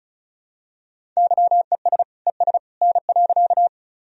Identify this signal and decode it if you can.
Morse code, heard as a keyed tone, about 34 wpm, 700 Hz. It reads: YES ES N.
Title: Sentences from Top 500 Words and Abbreviations 35wpm